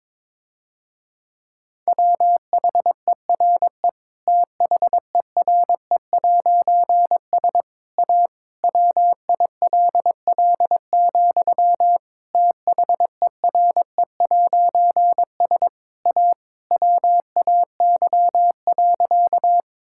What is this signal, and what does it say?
Morse code, heard as a keyed tone, about 22 wpm, 700 Hz. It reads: WHERE THERE'S A WILL, THERE'S A WAY.
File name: Difficult Sentences with Hints - 22wpm